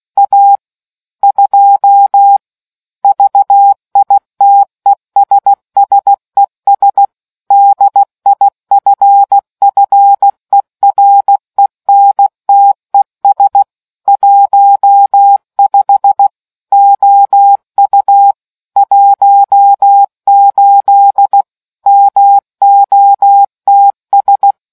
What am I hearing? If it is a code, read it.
Morse code, heard as a keyed tone, about 16 wpm, 800 Hz. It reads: A 2 VITESSES DIFFERENTES 15 OU 18 MOTS